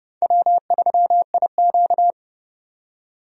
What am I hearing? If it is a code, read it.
Morse code, heard as a keyed tone, about 30 wpm, 700 Hz. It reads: W3IQ
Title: USA Call Signs - 4 Characters - 30wpm